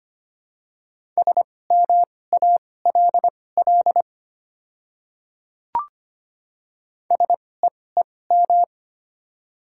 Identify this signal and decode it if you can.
Morse code, heard as a keyed tone, about 25 wpm, spaced as for 18 wpm, 700 Hz. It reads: SMALL SEEM
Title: 2x Character Spacing - Top 300 Words - 25wpm